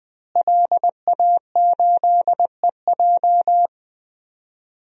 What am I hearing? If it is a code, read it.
Morse code, heard as a keyed tone, about 20 wpm, 700 Hz. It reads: LA8EJ
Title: DXCC-100 Call Signs - 5 Characters - Repeated 2x - 20wpm